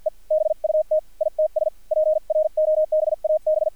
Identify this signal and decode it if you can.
Morse code, heard as a keyed tone, about 55 wpm, spaced as for 45 wpm, 620 Hz, with radio noise. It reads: E BUT ITS PROBAB
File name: archived / CRSnet